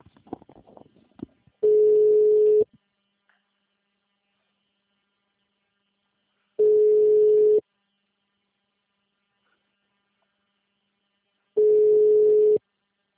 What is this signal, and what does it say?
Ringback tone (Europe, 425 Hz).
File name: out- -2022